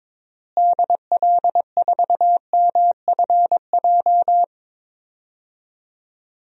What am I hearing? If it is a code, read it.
Morse code, heard as a keyed tone, about 21 wpm, 700 Hz. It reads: DL4MFJ
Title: DXCC-100 Call Signs - 6 Characters - Repeated 3x - 22wpm